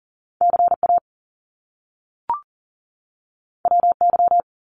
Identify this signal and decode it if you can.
Morse code, heard as a keyed tone, about 38 wpm, 700 Hz. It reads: CA WY
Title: US States 40wpm